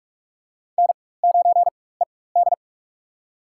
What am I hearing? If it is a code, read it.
Morse code, heard as a keyed tone, about 44 wpm, spaced as for 21 wpm, 700 Hz. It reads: N9ED
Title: USA Call Signs - 4x Character Spacing - 45wpm